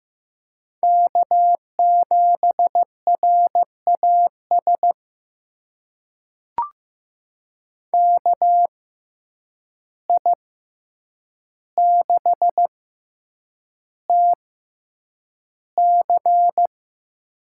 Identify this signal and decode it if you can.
Morse code, heard as a keyed tone, about 15 wpm, spaced as for 8 wpm, 700 Hz. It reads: K7RAS KI6TC
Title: USA Call Signs - 5 Characters - 6x Character Spacing - 15wpm